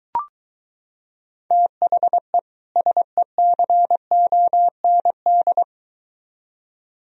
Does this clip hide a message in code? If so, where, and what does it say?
Yes: Morse code from 1.5 to 5.6 s, heard as a keyed tone, about 23 wpm, 700 Hz. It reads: THE SECOND